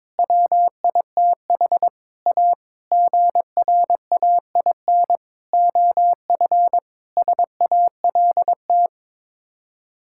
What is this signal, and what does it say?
Morse code, heard as a keyed tone, about 21 wpm, 700 Hz. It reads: WITH A GRAIN OF SALT
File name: practice / 22wpm / Sets of 7 Words - 22wpm